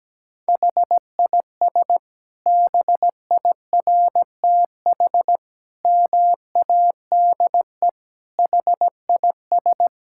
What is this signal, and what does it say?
Morse code, heard as a keyed tone, about 17 wpm, 700 Hz. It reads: HIS BIRTH MADE HIS
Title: Sentences from Top 1000 Words - Review Part 5 - 17wpm